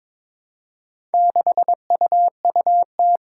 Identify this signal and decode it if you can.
Morse code, heard as a keyed tone, about 22 wpm, 700 Hz. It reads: 6UUT
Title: Four Challenging Letter-Number Combinations 22wpm